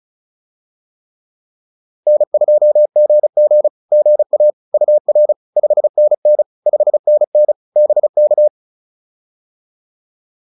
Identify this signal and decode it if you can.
Morse code, heard as a keyed tone, about 34 wpm, 590 Hz. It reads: N2GG GA UR 5NN 5NN BK
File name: POTA104 35